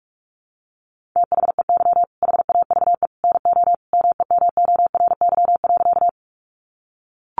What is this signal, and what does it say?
Morse code, heard as a keyed tone, about 44 wpm, 700 Hz. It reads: THEY HAVE NO MEMORY.
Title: Sentences from Top 1300 Words 45wpm